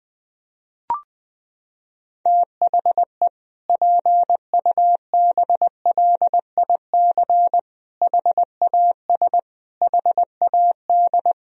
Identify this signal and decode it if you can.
Morse code, heard as a keyed tone, about 20 wpm, 700 Hz. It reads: THE PUBLIC HAS HAD
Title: Sentences from Top 1000 Words - Review Part 3 - 20wpm